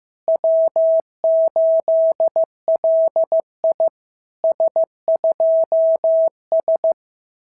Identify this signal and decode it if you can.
Morse code, heard as a keyed tone, about 14 wpm, 650 Hz. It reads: W8LI S2S
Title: SOTA QSO Practice 15wpm